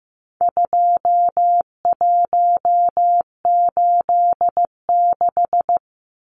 Morse code, heard as a keyed tone, about 15 wpm, 700 Hz. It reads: 2186
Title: Numbers - 4-Digit 15wpm